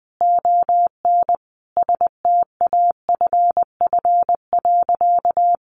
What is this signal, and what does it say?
Morse code, heard as a keyed tone, about 19 wpm, 700 Hz. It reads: ON STAFF.
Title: Sentences from Top 1200 Words 20wpm